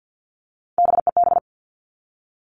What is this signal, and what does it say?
Morse code, heard as a keyed tone, about 50 wpm, 700 Hz. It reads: BEB
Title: All Cut Numbers - 3-Digit 50wpm